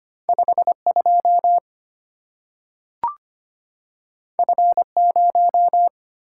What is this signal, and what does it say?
Morse code, heard as a keyed tone, about 25 wpm, 700 Hz. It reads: 52 F0
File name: Two Letter-Number Combinations - 25wpm